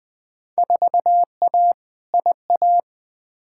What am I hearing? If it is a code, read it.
Morse code, heard as a keyed tone, about 20 wpm, 700 Hz. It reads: 4A IA
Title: Field Day Exchanges - 20wpm